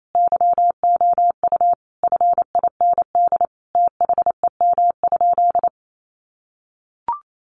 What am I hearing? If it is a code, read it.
Morse code, heard as a keyed tone, about 28 wpm, 700 Hz. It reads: YOU FIND THEM?
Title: Sentences from Top 200 Words 28wpm